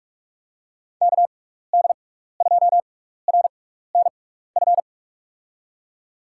Morse code, heard as a keyed tone, about 44 wpm, spaced as for 15 wpm, 700 Hz. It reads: KD2RNF
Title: USA Call Signs - 6 Characters - 6x Character Spacing - 45wpm